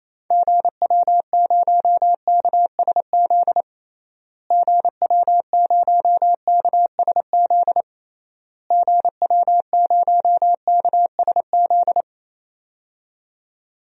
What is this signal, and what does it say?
Morse code, heard as a keyed tone, about 28 wpm, 700 Hz. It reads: GW0KSZ GW0KSZ GW0KSZ
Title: DXCC-100 Call Signs - 6 Characters - Repeated 3x - 28wpm